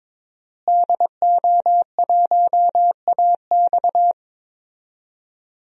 Morse code, heard as a keyed tone, about 21 wpm, 700 Hz. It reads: DO1AX
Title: DXCC-100 Call Signs - 5 Characters - Repeated 3x - 22wpm